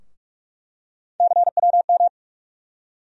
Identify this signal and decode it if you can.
Morse code, heard as a keyed tone, about 46 wpm, 700 Hz. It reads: CWM
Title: USA Call Sign Postfixes - 3 Letters 45wpm